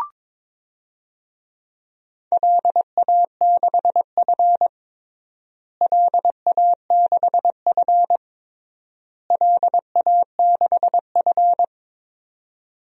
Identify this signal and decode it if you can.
Morse code, heard as a keyed tone, about 22 wpm, 700 Hz. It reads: LA6F LA6F LA6F
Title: DXCC-100 Call Signs - 4 Characters - Repeated 3x - 22wpm